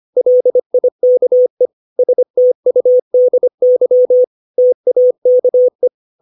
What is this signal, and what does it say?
Morse code, heard as a keyed tone, about 25 wpm, 500 Hz. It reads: LIKE STUDY TAKE
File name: 632 words / 632 Words 25WPM